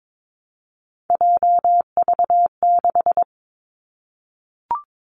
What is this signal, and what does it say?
Morse code, heard as a keyed tone, about 22 wpm, 700 Hz. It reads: JV6
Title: Three Challenging Letter-Number Combinations 22wpm